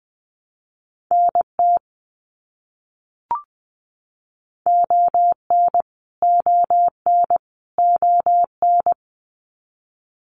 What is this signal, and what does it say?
Morse code, heard as a keyed tone, about 20 wpm, 700 Hz. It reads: NT ON ON ON
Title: Canadian Provinces and Territories - Repeated 3x - 20wpm